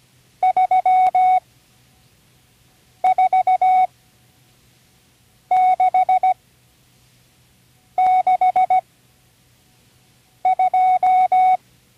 Morse code, heard as a keyed tone, about 17 wpm, spaced as for 5 wpm, 740 Hz. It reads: 34662
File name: CW Learning Programs / Learning Morse Code (from Tune in the World) / 37 all numbers